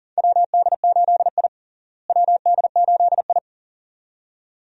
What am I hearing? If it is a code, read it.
Morse code, heard as a keyed tone, about 40 wpm, 700 Hz. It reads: WD8I WD8I
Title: USA Call Signs - 4 Characters - Repeated 2x - 40wpm